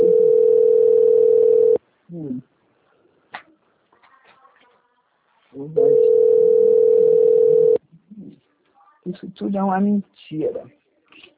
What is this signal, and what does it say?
Ringback tone (North America), 440 + 480 Hz, with speech.
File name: out- - - - - .43158